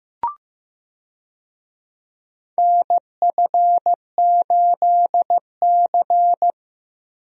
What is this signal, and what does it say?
Morse code, heard as a keyed tone, about 15 wpm, 700 Hz. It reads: NF8C
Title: USA Call Signs - 4 Characters - Repeated 3x - 15wpm